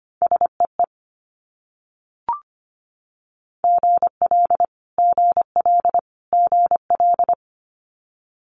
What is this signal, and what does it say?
Morse code, heard as a keyed tone, about 25 wpm, 700 Hz. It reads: SEE GL GL GL
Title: Multi-letter Phonemes Repeated 3x 25wpm